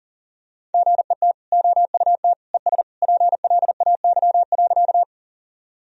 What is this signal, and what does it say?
Morse code, heard as a keyed tone, about 40 wpm, 700 Hz. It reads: GET OUT ES PLAY.
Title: Sentences from Top 200 Words and Abbreviations 40wpm